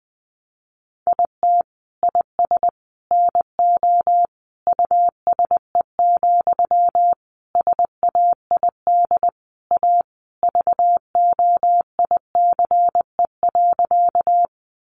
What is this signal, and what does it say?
Morse code, heard as a keyed tone, about 20 wpm, 700 Hz. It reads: IT IS NO USE, SAID A VOICE.
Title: Sentences from Top 500 Words 20wpm